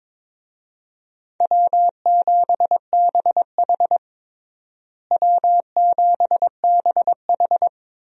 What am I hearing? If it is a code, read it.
Morse code, heard as a keyed tone, about 22 wpm, 700 Hz. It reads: W7BH W7BH
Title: USA Call Signs - 4 Characters - Repeated 2x - 22wpm